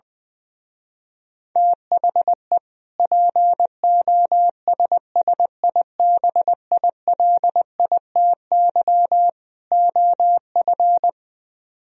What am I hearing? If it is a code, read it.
Morse code, heard as a keyed tone, about 20 wpm, 700 Hz. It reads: THE POSSIBILITY OF
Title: Speed-Racing - Sets of 3 Words 20wpm